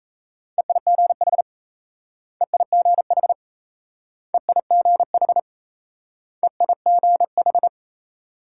Morse code, heard as a keyed tone, about 38 wpm, 700 Hz. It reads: EIGH EIGH EIGH EIGH